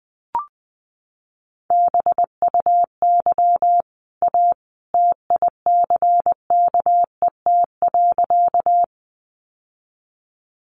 Morse code, heard as a keyed tone, about 20 wpm, 700 Hz. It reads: BUY A TICKET.
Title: Sentences from Top 1400 Words 20wpm